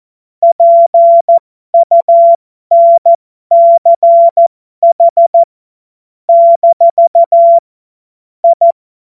Morse code, heard as a keyed tone, about 14 wpm, spaced as for 12 wpm, 680 Hz. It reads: PUNCH - I
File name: Morsekurs / Witze